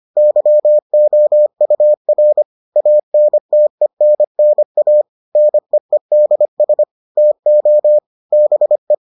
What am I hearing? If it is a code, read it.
Morse code, heard as a keyed tone, about 25 wpm, 600 Hz. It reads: YOUR ANTENNA NEEDS TO BE